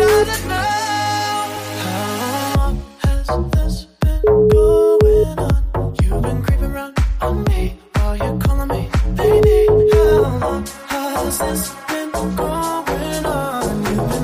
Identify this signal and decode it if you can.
Ringback tone (Europe, 425 Hz), with noise.